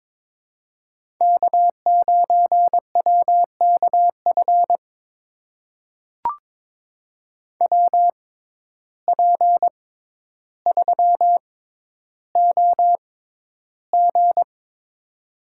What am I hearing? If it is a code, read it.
Morse code, heard as a keyed tone, about 22 wpm, spaced as for 11 wpm, 700 Hz. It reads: K9WKF WP3OG
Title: USA Call Signs - 5 Characters - 6x Character Spacing - 22wpm